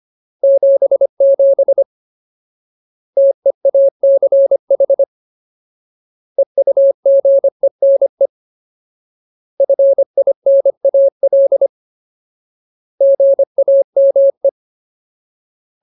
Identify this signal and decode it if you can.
Morse code, heard as a keyed tone, about 25 wpm, 550 Hz. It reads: 77 TEACH EUGENE FINAL GAME
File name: Ultimate Word List / 25WPM Ultimate